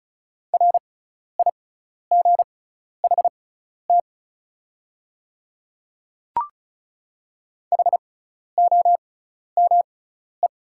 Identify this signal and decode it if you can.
Morse code, heard as a keyed tone, about 36 wpm, spaced as for 12 wpm, 690 Hz. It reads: RIGHT HOME
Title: 6x Character Spacing - Top 200 Words - 35wpm